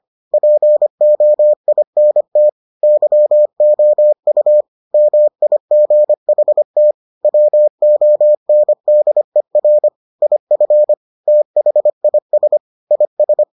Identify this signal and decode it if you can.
Morse code, heard as a keyed tone, about 25 wpm, 600 Hz. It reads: POINT YOU MIGHT WONDER IF THIS IS